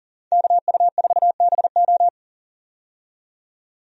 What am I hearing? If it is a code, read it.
Morse code, heard as a keyed tone, about 37 wpm, 700 Hz. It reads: KU4BO
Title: USA Call Signs - 5 Characters - 40wpm